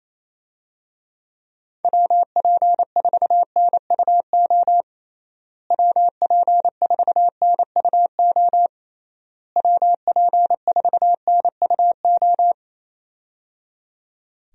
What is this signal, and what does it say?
Morse code, heard as a keyed tone, about 27 wpm, 700 Hz. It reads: WP4NUO WP4NUO WP4NUO
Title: USA Call Signs - 6 Characters - Repeated 3x - 28wpm